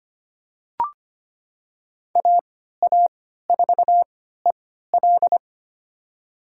Morse code, heard as a keyed tone, about 25 wpm, spaced as for 14 wpm, 700 Hz. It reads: AA4EL
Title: USA Call Signs - 3x Character Spacing - 25wpm